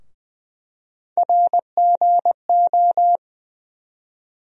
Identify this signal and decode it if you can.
Morse code, heard as a keyed tone, about 20 wpm, 700 Hz. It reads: RGO
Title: USA Call Sign Postfixes - 3 Letters 20wpm